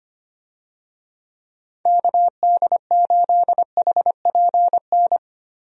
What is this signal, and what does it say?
Morse code, heard as a keyed tone, about 25 wpm, 700 Hz. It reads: KD8HPN
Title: USA Call Signs - 6 Characters - Repeated 3x - 25wpm